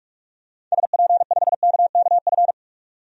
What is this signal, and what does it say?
Morse code, heard as a keyed tone, about 46 wpm, 700 Hz. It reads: SP5KKF
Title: DXCC-100 Call Signs - 6 Characters - 45wpm